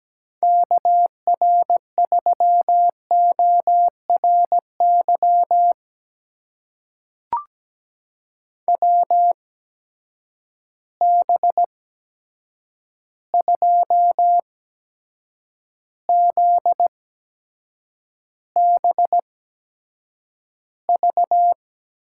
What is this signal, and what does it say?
Morse code, heard as a keyed tone, about 17 wpm, spaced as for 7 wpm, 700 Hz. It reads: KR3ORY WB2ZBV
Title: USA Call Signs - 6 Characters - 8x Character Spacing - 17wpm